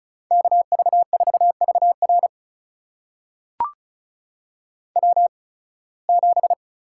Morse code, heard as a keyed tone, about 36 wpm, spaced as for 23 wpm, 700 Hz. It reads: KV4VR W7
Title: USA Call Signs - 5 Characters - 8x Character Spacing - 35wpm